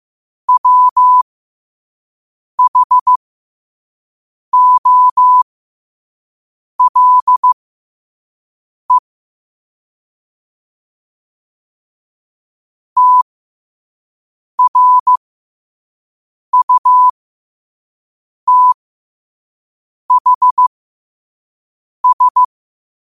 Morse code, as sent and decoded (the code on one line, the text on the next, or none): .-- .... --- .-.. / - .-. ..- - .... ...
WHOLE TRUTHS